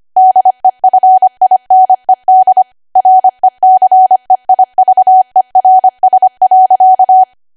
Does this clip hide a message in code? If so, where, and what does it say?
Yes: Morse code from 0.2 to 7.2 s, heard as a keyed tone, about 25 wpm, 750 Hz. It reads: DEFINED RECEIVERS.